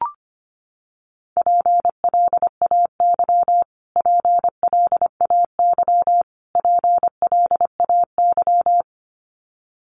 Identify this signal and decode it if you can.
Morse code, heard as a keyed tone, about 25 wpm, 700 Hz. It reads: PLAY PLAY PLAY
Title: Top 200 Words Repeated 25wpm